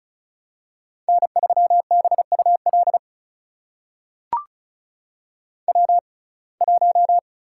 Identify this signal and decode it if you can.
Morse code, heard as a keyed tone, about 34 wpm, spaced as for 25 wpm, 700 Hz. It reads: N3BUL W1
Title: USA Call Signs - 5 Characters - 6x Character Spacing - 35wpm